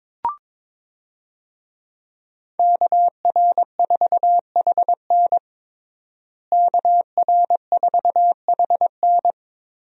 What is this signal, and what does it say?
Morse code, heard as a keyed tone, about 22 wpm, 700 Hz. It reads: KR4HN KR4HN
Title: USA Call Signs - 5 Characters - Repeated 3x - 22wpm